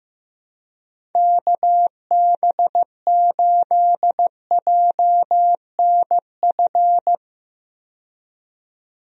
Morse code, heard as a keyed tone, about 15 wpm, 700 Hz. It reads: KB8JNF
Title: USA Call Signs - 6 Characters - 15wpm